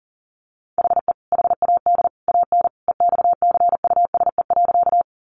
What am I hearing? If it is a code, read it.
Morse code, heard as a keyed tone, about 40 wpm, 700 Hz. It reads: HE HAD AN EXCUSE.
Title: Sentences from Top 800 Words 40wpm